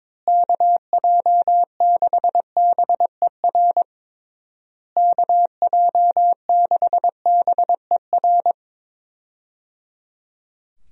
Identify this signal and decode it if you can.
Morse code, heard as a keyed tone, about 22 wpm, 700 Hz. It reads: KJ6BER KJ6BER
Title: USA Call Signs - 6 Characters - Repeated 3x - 22wpm